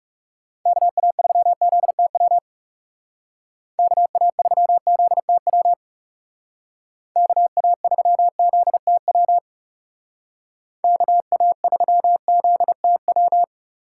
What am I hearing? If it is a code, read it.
Morse code, heard as a keyed tone, about 36 wpm, 700 Hz. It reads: KA3ZTW KA3ZTW KA3ZTW KA3ZTW
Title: Speed Racing - USA 2x3 Call Signs 30wpm